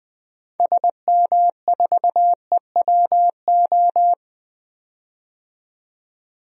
Morse code, heard as a keyed tone, about 20 wpm, 700 Hz. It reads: SM4EWO
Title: DXCC-100 Call Signs - 6 Characters - 20wpm